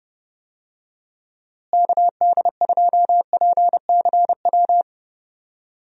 Morse code, heard as a keyed tone, about 31 wpm, 700 Hz. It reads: KD2PCW
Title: USA Call Signs - 6 Characters - Repeated 2x - 30wpm